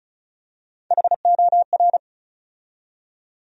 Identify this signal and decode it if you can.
Morse code, heard as a keyed tone, about 36 wpm, 700 Hz. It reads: HOR
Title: USA Call Sign Postfixes - 3 Letters 35wpm